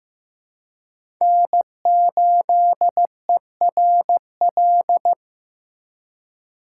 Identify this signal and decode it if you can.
Morse code, heard as a keyed tone, about 15 wpm, 700 Hz. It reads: N8ERL
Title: USA Call Signs - 5 Characters - 4x Character Spacing - 15wpm